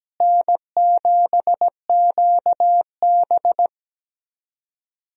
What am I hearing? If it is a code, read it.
Morse code, heard as a keyed tone, about 17 wpm, 700 Hz. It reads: N7QB
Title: USA Call Signs - 4 Characters - Repeated 3x - 17wpm